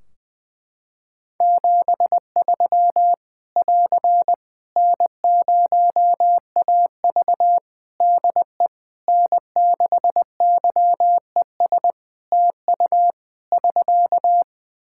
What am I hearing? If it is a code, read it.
Morse code, heard as a keyed tone, about 20 wpm, 700 Hz. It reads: 73 <AR> N0AV DE N6YES TU <SK>
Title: Practice QSO - Exchange 4 - Ending - 20wpm